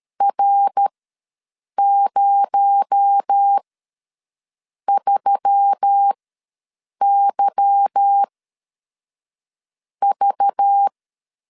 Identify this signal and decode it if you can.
Morse code, heard as a keyed tone, about 13 wpm, spaced as for 6 wpm, 800 Hz. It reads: R03YV
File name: LEZIONI / 8 YJ45 / 8 YJ ws 12wpm 3342